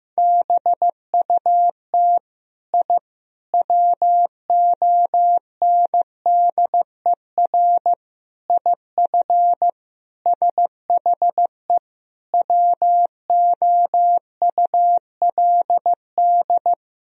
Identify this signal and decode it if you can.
Morse code, heard as a keyed tone, about 15 wpm, 700 Hz. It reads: BUT I WONDER IF SHE WOULD
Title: Sentences from Top 1000 Words - Review Part 3 - 15wpm